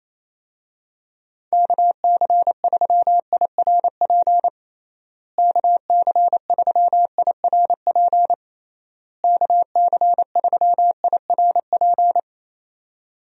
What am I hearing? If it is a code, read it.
Morse code, heard as a keyed tone, about 27 wpm, 700 Hz. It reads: KC3IRP KC3IRP KC3IRP
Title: USA Call Signs - 6 Characters - Repeated 3x - 28wpm